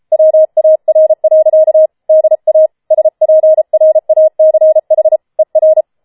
Morse code, heard as a keyed tone, about 33 wpm, 630 Hz. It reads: WAR. DA SPRACH ER
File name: Schneewittchen Teil5 34wpm